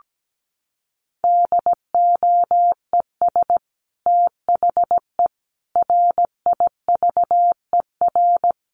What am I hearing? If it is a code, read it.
Morse code, heard as a keyed tone, about 17 wpm, 700 Hz. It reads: DOES THE RIVER
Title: Sentences from Top 1200 Words 17wpm